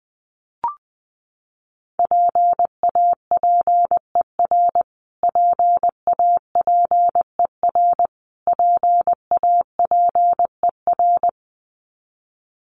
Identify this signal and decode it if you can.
Morse code, heard as a keyed tone, about 20 wpm, 700 Hz. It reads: PAPER PAPER PAPER
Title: Top 500 Words Repeated 20wpm